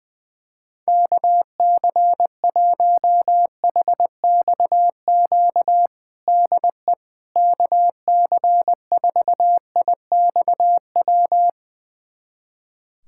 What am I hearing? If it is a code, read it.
Morse code, heard as a keyed tone, about 20 wpm, 700 Hz. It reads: KC1HXQ DE KC4IXW
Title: USA Callsign DE Callsign - 6 Characters - 20wpm